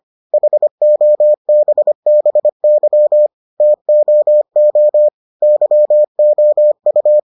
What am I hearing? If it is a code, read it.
Morse code, heard as a keyed tone, about 25 wpm, 600 Hz. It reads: HOBBY TOO YOU